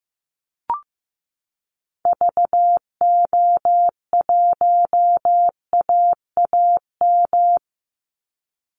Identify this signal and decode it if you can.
Morse code, heard as a keyed tone, about 15 wpm, 700 Hz. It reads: VO1AAM
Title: Call Signs Repeated 3x - 15wpm